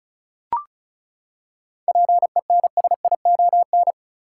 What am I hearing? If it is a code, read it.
Morse code, heard as a keyed tone, about 36 wpm, 700 Hz. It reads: PENSION
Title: Seven Letter Words - 35wpm